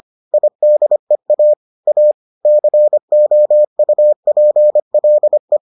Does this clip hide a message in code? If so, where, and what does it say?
Yes: Morse code from 0.3 to 5.6 s, heard as a keyed tone, about 25 wpm, 600 Hz. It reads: IDEA A COUPLE